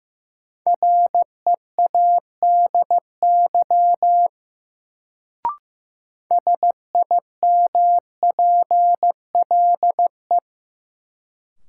Morse code, heard as a keyed tone, about 16 wpm, 700 Hz. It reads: READY SIMPLE
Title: Sentences from Top 400 Words - New Word Drill - 15wpm